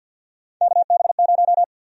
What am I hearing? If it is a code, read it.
Morse code, heard as a keyed tone, about 52 wpm, 700 Hz. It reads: KB0
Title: USA Call Sign Prefixes - 50wpm